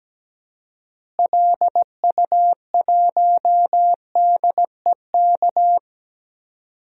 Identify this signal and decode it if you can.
Morse code, heard as a keyed tone, about 17 wpm, 700 Hz. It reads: LU1DEK
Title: DXCC-100 Call Signs - 6 Characters - Repeated 3x - 17wpm